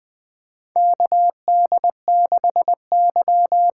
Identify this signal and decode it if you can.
Morse code, heard as a keyed tone, about 21 wpm, 700 Hz. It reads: KD6Y
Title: USA Call Signs - 6 Characters - 8x Character Spacing - 20wpm